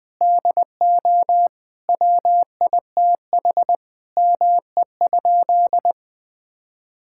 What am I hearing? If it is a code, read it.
Morse code, heard as a keyed tone, about 21 wpm, 700 Hz. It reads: DO WITH ME?
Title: Sentences with 5-Word Phrases - 20wpm